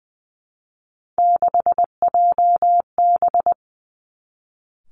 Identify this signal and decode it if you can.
Morse code, heard as a keyed tone, about 20 wpm, 700 Hz. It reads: 6JB
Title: Three Challenging Letter-Number Combinations 20wpm